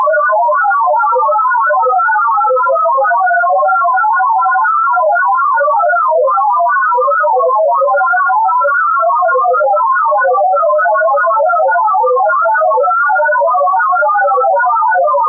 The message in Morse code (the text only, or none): none